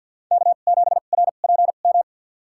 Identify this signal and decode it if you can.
Morse code, heard as a keyed tone, about 52 wpm, spaced as for 37 wpm, 700 Hz. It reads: K7RPM
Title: USA Call Signs - 5 Characters - 2x Character Spacing - 50wpm